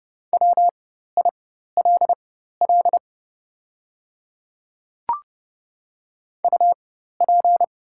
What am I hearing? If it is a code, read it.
Morse code, heard as a keyed tone, about 30 wpm, spaced as for 14 wpm, 700 Hz. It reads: WILL UP
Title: 4x Character Spacing - Top 200 Words - 30wpm